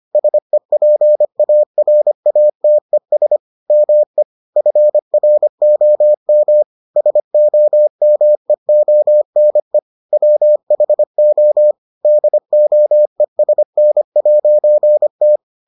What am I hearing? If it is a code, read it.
Morse code, heard as a keyed tone, about 25 wpm, 600 Hz. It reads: SEPARATES ME FROM SOMEONE WHO DOESN'T